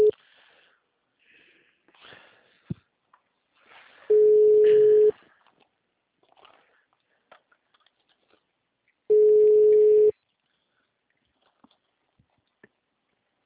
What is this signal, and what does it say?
Ringback tone (Europe, 425 Hz).
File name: out- -2002- - - .16